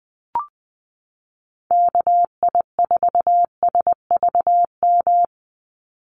Morse code, heard as a keyed tone, about 20 wpm, 700 Hz. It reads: KI4SVM
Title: NA SOTA Activator Callsigns 20wpm